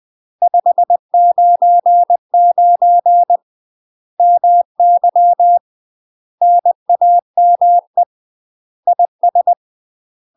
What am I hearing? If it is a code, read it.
Morse code, heard as a keyed tone, about 20 wpm, 700 Hz. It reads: 599 MY NAME IS
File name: UK MASTER QSOs TRAINING/20WPM